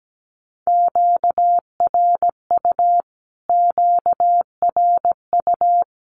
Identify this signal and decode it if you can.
Morse code, heard as a keyed tone, about 17 wpm, 700 Hz. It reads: QRU QRU
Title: Q-Codes Repeated 3x 17wpm